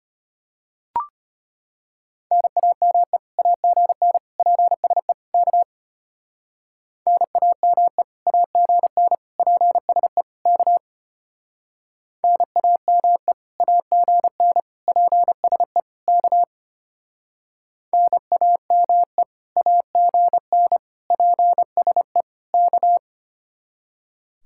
Morse code, heard as a keyed tone, about 31 wpm, 700 Hz. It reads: NAME AGN PSE K NAME AGN PSE K NAME AGN PSE K NAME AGN PSE K